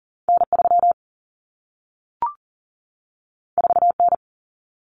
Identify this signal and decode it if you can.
Morse code, heard as a keyed tone, about 40 wpm, 700 Hz. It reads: N3 4N
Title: Common Cut Numbers - 2-Digit 40wpm